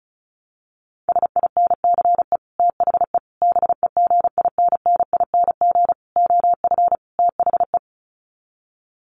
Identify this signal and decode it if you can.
Morse code, heard as a keyed tone, about 34 wpm, 700 Hz. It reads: SINCE THE BEGINNING OF THE
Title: Sets of 5 Words Encore2 35wpm